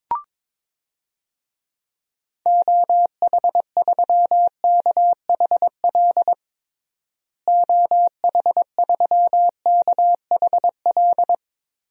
Morse code, heard as a keyed tone, about 22 wpm, 700 Hz. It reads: OH3KHL OH3KHL
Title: DXCC-100 Call Signs - 6 Characters - Repeated 3x - 22wpm